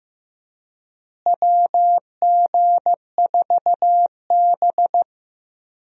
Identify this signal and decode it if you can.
Morse code, heard as a keyed tone, about 15 wpm, 700 Hz. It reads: WG4B
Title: USA Call Signs - 4 Characters - Repeated 2x - 15wpm